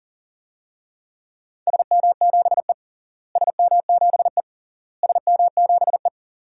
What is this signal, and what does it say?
Morse code, heard as a keyed tone, about 40 wpm, 700 Hz. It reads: SM7E SM7E SM7E
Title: DXCC-100 Call Signs - 4 Characters - Repeated 3x - 40wpm